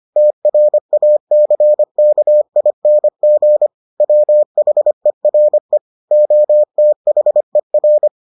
Morse code, heard as a keyed tone, about 25 wpm, 600 Hz. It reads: TRACKING WHERE OTHER